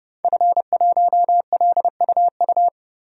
Morse code, heard as a keyed tone, about 31 wpm, 700 Hz. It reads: F1LUU
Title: DXCC-100 Call Signs - 5 Characters - 30wpm